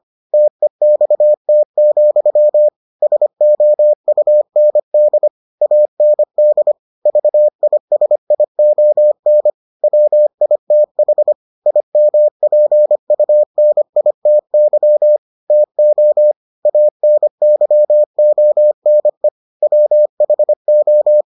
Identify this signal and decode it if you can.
Morse code, heard as a keyed tone, about 25 wpm, 600 Hz. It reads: TEXT, SOUND AND VISION WITH IMPUNITY TO ANYONE WHO